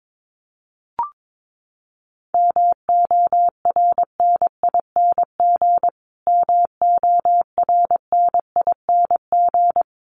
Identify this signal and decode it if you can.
Morse code, heard as a keyed tone, about 22 wpm, 700 Hz. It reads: MORNING MORNING
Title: Top 500 Words Repeated 22wpm